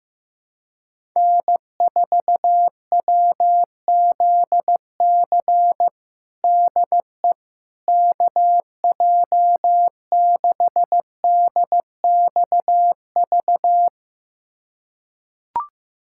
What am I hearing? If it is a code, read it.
Morse code, heard as a keyed tone, about 15 wpm, 700 Hz. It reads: N4WZC DE KJ6DXV
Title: USA Callsign DE Callsign 15wpm